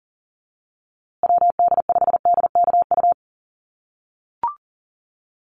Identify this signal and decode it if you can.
Morse code, heard as a keyed tone, about 40 wpm, 700 Hz. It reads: WD5DKU
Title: USA Call Signs Repeated 3x 40wpm